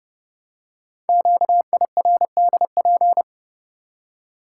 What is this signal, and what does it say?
Morse code, heard as a keyed tone, about 30 wpm, 700 Hz. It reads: QIRDP
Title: Code Groups - 5 Letter:Numbers - 30wpm